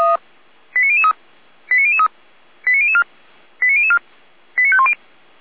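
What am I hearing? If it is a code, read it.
Five-tone selective call (ZVEI), heard as quick runs of five tones, tones of about 70 ms. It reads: 89003 89003 89004 89004 89510
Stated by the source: ZVEI five-tone selective call received by SDR, logged by Feuerwehr Orlamünde.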